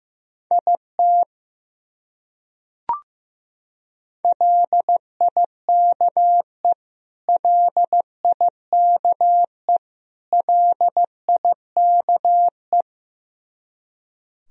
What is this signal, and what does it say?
Morse code, heard as a keyed tone, about 15 wpm, 700 Hz. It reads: IT LIKE LIKE LIKE
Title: Top 100 Words Repeated 15wpm